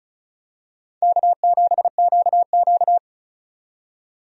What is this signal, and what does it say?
Morse code, heard as a keyed tone, about 34 wpm, 700 Hz. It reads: K7QQ
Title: USA Call Signs - 4 Characters - 35wpm